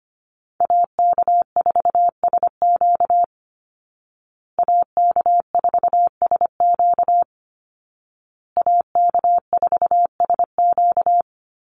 Morse code, heard as a keyed tone, about 25 wpm, 700 Hz. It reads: AK4SQ AK4SQ AK4SQ
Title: Call Signs Repeated 2x - 25wpm